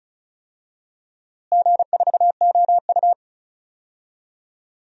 Morse code, heard as a keyed tone, about 36 wpm, 700 Hz. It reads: G4OU
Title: DXCC-100 Call Signs - 4 Characters - Repeated 2x - 35wpm